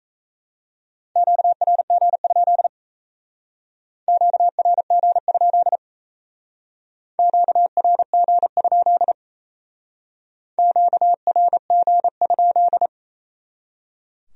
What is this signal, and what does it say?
Morse code, heard as a keyed tone, about 34 wpm, 700 Hz. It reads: QRG? QRG? QRG? QRG?
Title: Speed-Racing Q-Codes - 28wpm